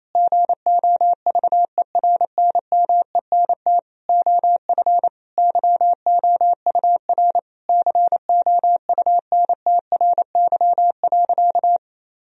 Morse code, heard as a keyed tone, about 28 wpm, 700 Hz. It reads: GOVERNMENT OF YOUR COUNTRY.
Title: Sentences from Top 1000 Words - Review Part 3 - 28wpm